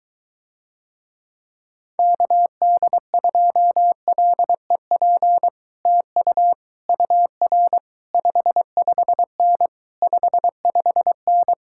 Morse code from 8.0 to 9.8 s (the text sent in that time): 55N